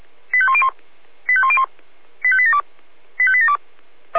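Five-tone selective call (ZVEI), heard as quick runs of five tones, tones of about 70 ms. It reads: 85101 85101 85872 85872
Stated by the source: ZVEI five-tone call received on an SDR.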